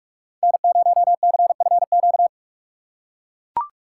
Morse code, heard as a keyed tone, about 44 wpm, 700 Hz. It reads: N0CFQ